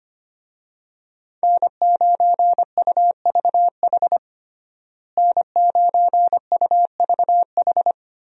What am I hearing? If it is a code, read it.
Morse code, heard as a keyed tone, about 25 wpm, 700 Hz. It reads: N9UVH N9UVH
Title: USA Call Signs - 5 Characters - Repeated 3x - 25wpm